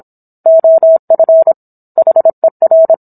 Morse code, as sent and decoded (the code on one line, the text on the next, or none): --- ..-. / .... .-.
OF HER